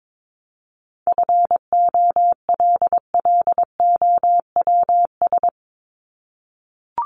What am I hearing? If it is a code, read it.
Morse code, heard as a keyed tone, about 22 wpm, 700 Hz. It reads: FOLLOWS
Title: ES:S Suffix - Top 100 Words with Base Words 22wpm